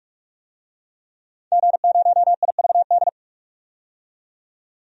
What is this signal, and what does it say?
Morse code, heard as a keyed tone, about 46 wpm, 700 Hz. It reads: G0IVD
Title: DXCC-100 Call Signs - 5 Characters - 45wpm